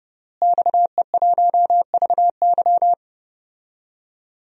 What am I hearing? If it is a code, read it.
Morse code, heard as a keyed tone, about 31 wpm, 700 Hz. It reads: XE1VY
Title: DXCC-100 Call Signs - 5 Characters - Repeated 2x - 30wpm